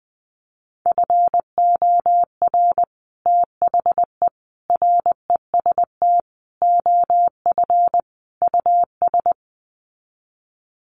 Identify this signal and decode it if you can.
Morse code, heard as a keyed tone, about 20 wpm, 700 Hz. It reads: FOR THE REST OF US
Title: Sets of 5 Words Encore 20wpm